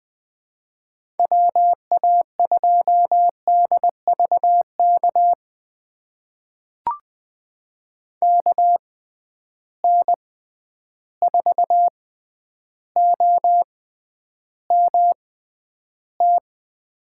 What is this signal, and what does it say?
Morse code, heard as a keyed tone, about 21 wpm, spaced as for 10 wpm, 700 Hz. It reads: WA2DVK KN4OMT
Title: USA Call Signs - 6 Characters - 6x Character Spacing - 20wpm